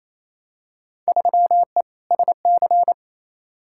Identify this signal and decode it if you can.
Morse code, heard as a keyed tone, about 28 wpm, 700 Hz. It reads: 3E SC
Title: Field Day Exchanges Encore - 28wpm